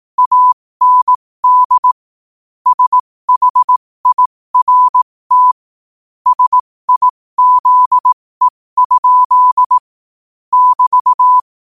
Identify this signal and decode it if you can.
Morse code, heard as a keyed tone, about 18 wpm, spaced as for 15 wpm, 1000 Hz. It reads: AND SHIRT SIZE? <BT>
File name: qotdfiles / 15wpm